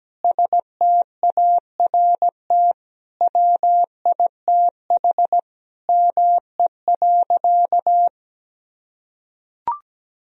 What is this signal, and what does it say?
Morse code, heard as a keyed tone, about 17 wpm, 700 Hz. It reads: START WITH ME.
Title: Sentences from Top 1000 Words - Review Part 1 - 17wpm